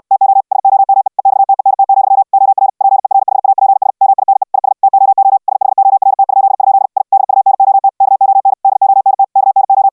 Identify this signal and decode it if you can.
Morse code, heard as a keyed tone, about 69 wpm, 780 Hz. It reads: TO SOME EXTENT, OR PERHAPS MINE IS TOO SHORT?? I DUNOT KON DONT KNO